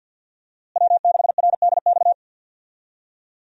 Morse code, heard as a keyed tone, about 47 wpm, 700 Hz. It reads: W6RDX
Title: USA Call Signs - 5 Characters - Repeated 2x - 50wpm